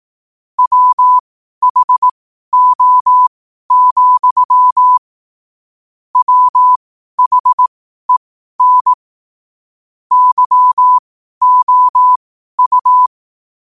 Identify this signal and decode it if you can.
Morse code, heard as a keyed tone, about 18 wpm, spaced as for 12 wpm, 1000 Hz. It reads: WHO, WHEN YOU
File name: qotdfiles / 12wpm